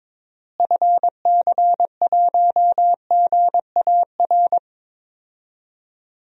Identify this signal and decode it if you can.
Morse code, heard as a keyed tone, about 22 wpm, 700 Hz. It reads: FC1GAR
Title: DXCC-100 Call Signs - 6 Characters - Repeated 3x - 22wpm